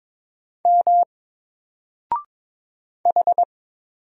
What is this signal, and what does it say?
Morse code, heard as a keyed tone, about 22 wpm, 700 Hz. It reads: M H
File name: Single-Letter-Number-22wpm